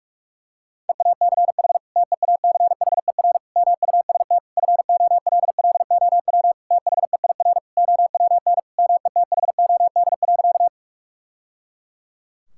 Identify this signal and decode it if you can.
Morse code, heard as a keyed tone, about 46 wpm, 700 Hz. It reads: EACH TEACHER MUST FOLLOW THEIR OWN METHOD.
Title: Sentences from Top 1000 Words - Review Part 6 - 45wpm